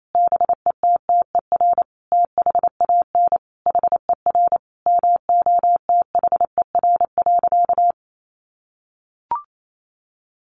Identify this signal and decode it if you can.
Morse code, heard as a keyed tone, about 28 wpm, 700 Hz. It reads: BETTER THAN HER MOTHER.
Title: Sentences from Top 400 Words 28wpm